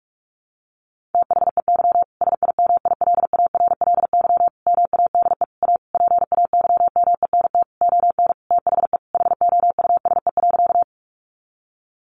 Morse code, heard as a keyed tone, about 44 wpm, 700 Hz. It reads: THEY SIMILARLY MADE A PAYMENT ON THE HOUSE.
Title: Sentences from Top 1400 Words 45wpm